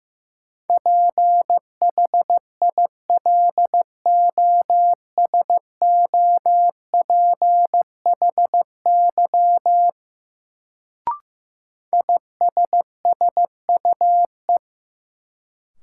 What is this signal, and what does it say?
Morse code, heard as a keyed tone, about 15 wpm, 700 Hz. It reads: PHILOSOPHY ISSUE
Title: Sentences from Top 900 Words - New Word Drill - 15wpm